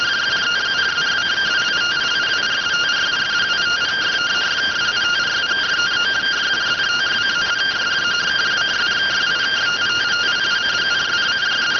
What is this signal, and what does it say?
Two-tone teleprinter signal, about 170 Hz shift.